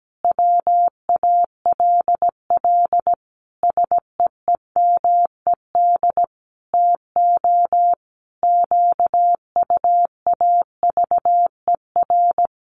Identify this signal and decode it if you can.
Morse code, heard as a keyed tone, about 17 wpm, 700 Hz. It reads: WALL SEEMED TO QUAVER
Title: A Sound Of Thunder 17wpm